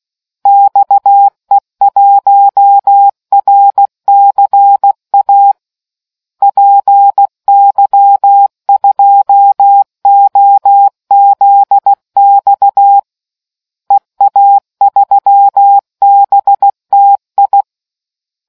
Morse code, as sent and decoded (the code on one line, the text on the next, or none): -..- . .---- .-. -.-. .- / .--. -.-- ..--- --- --.. -..- / . .- ...-- -... - ..
XE1RCA PY2OZX EA3BTI